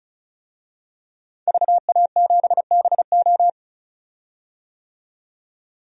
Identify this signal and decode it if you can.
Morse code, heard as a keyed tone, about 36 wpm, 700 Hz. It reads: VA7BO
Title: DXCC-100 Call Signs - 5 Characters - 35wpm